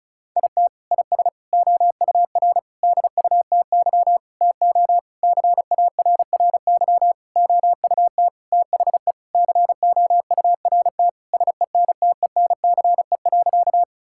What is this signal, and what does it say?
Morse code, heard as a keyed tone, about 36 wpm, 700 Hz. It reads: IT IS OUR DUTY TO CARRY OUT THE COURT SENTENCE.
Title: Sentences from Top 1600 Words 35wpm